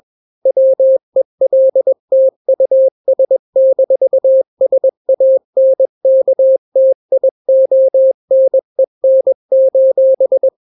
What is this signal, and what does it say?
Morse code, heard as a keyed tone, about 21 wpm, 530 Hz. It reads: WELT US-SANKTIONEN:
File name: daily / 20wpm